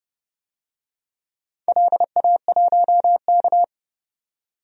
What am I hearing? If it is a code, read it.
Morse code, heard as a keyed tone, about 30 wpm, 700 Hz. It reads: LA1K
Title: DXCC-100 Call Signs - 4 Characters - 30wpm